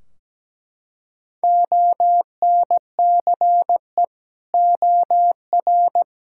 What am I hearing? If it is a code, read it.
Morse code, heard as a keyed tone, about 17 wpm, 700 Hz. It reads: ONCE OR
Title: Sentences from Top 1600 Words 17wpm